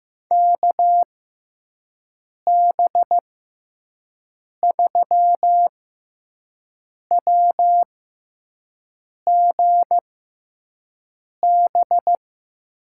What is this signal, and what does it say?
Morse code, heard as a keyed tone, about 15 wpm, spaced as for 5 wpm, 700 Hz. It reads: KB3WGB
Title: USA Call Signs - 6 Characters - 6x Character Spacing - 15wpm